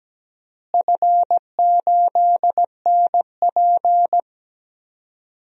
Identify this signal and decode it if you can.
Morse code, heard as a keyed tone, about 17 wpm, 700 Hz. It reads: F8NP
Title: DXCC-100 Call Signs - 4 Characters - Repeated 3x - 17wpm